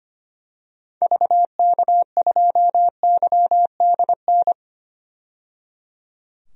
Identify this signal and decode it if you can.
Morse code, heard as a keyed tone, about 25 wpm, 700 Hz. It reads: VK2YDN
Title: DXCC-100 Call Signs - 6 Characters - 25wpm